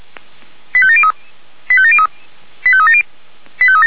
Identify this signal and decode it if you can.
Five-tone selective call (ZVEI), heard as quick runs of five tones, tones of about 70 ms. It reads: 85703 85703 85370
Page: /464/SDR/ZVEI/89234/